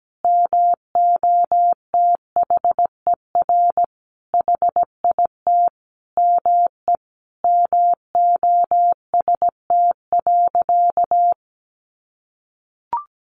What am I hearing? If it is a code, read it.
Morse code, heard as a keyed tone, about 17 wpm, 700 Hz. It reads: MOTHER HIT ME MOST.
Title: Sentences from Top 1400 Words 17wpm